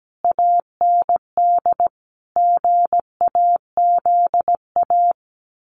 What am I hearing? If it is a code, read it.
Morse code, heard as a keyed tone, about 17 wpm, 700 Hz. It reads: AND GAZA
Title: Sets of 5 Words 17wpm